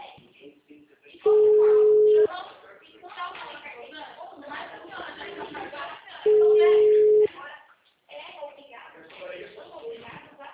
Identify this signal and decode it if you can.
Ringback tone (Europe, 425 Hz), with speech.